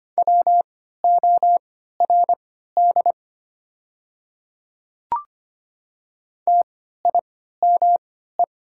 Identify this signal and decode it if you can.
Morse code, heard as a keyed tone, about 25 wpm, spaced as for 14 wpm, 700 Hz. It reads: WORD TIME
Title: 3x Character Spacing - Top 100 Words - 25wpm